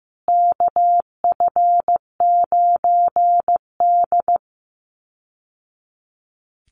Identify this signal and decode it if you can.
Morse code, heard as a keyed tone, about 15 wpm, 700 Hz. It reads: KF9D
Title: Call Signs Repeated 2x - 15wpm